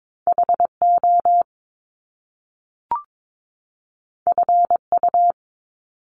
Morse code, heard as a keyed tone, about 22 wpm, 700 Hz. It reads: HO FU
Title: Top Two Letter Combinations in Words 22wpm